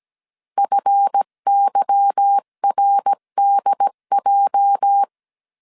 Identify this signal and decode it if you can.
Morse code, heard as a keyed tone, about 17 wpm, 800 Hz. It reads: FYRDJ
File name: CW / DET / 1 9 1500C S B 16wpm 2856